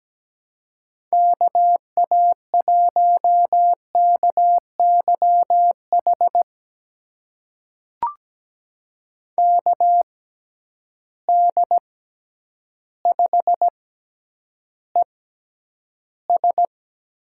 Morse code, heard as a keyed tone, about 17 wpm, spaced as for 9 wpm, 700 Hz. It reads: KA1KYH KD5ES
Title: USA Call Signs - 6 Characters - 6x Character Spacing - 17wpm